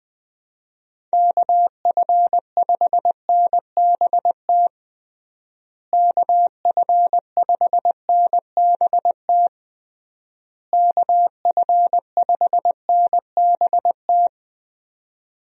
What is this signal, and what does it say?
Morse code, heard as a keyed tone, about 20 wpm, 700 Hz. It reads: KF5NBT KF5NBT KF5NBT
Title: USA Call Signs - 6 Characters - Repeated 3x - 20wpm